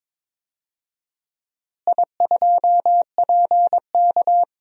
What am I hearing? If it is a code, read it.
Morse code, heard as a keyed tone, about 21 wpm, 700 Hz. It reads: I2PK
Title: DXCC-100 Call Signs - 4 Characters - Repeated 3x - 22wpm